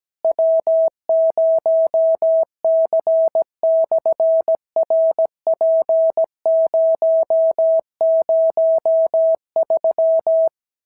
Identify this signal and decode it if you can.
Morse code, heard as a keyed tone, about 17 wpm, 650 Hz. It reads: W0C/RP003
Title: SOTA QSO Practice 17wpm